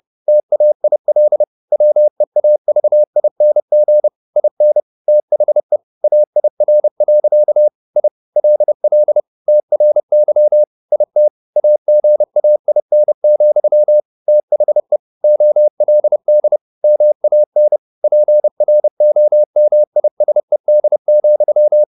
Morse code, heard as a keyed tone, about 30 wpm, 600 Hz. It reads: TAIL WEAVING IN THE AIR. I LL TRY IT AGAIN, THE OLD MAN PROMISED,